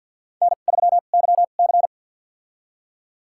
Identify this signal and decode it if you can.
Morse code, heard as a keyed tone, about 50 wpm, spaced as for 36 wpm, 700 Hz. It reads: N3YC
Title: USA Call Signs - 4 Characters - 2x Character Spacing - 50wpm